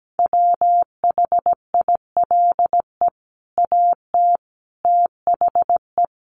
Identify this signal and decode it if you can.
Morse code, heard as a keyed tone, about 17 wpm, 700 Hz. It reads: WHILE AT THE
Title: Sets of 5 Words 17wpm